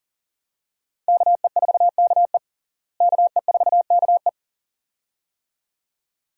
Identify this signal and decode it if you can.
Morse code, heard as a keyed tone, about 40 wpm, 700 Hz. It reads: KE4KE KE4KE
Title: USA Call Signs - 5 Characters - Repeated 2x - 40wpm